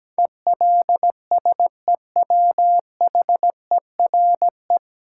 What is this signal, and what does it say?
Morse code, heard as a keyed tone, about 17 wpm, 700 Hz. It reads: ELSEWHERE
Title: Sentences from Top 1600 Words - New Word Drill Alt - 17wpm